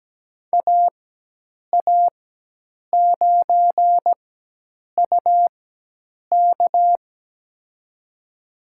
Morse code, heard as a keyed tone, about 17 wpm, spaced as for 8 wpm, 700 Hz. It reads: AA9UK
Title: USA Call Signs - 5 Characters - 4x Character Spacing - 17wpm